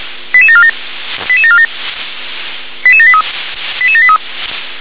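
Five-tone selective call (ZVEI), heard as quick runs of five tones, tones of about 70 ms. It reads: 88747 88747 88783 88783
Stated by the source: ZVEI five-tone selective call received on SDR.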